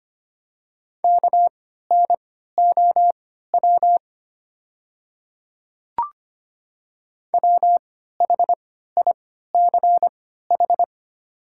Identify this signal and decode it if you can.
Morse code, heard as a keyed tone, about 25 wpm, spaced as for 14 wpm, 690 Hz. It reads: KNOW WHICH
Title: 3x Character Spacing - Top 100 Words - 25wpm